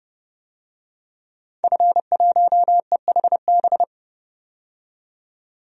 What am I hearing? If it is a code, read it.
Morse code, heard as a keyed tone, about 29 wpm, 700 Hz. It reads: F1EHB
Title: DXCC-100 Call Signs - 5 Characters - 30wpm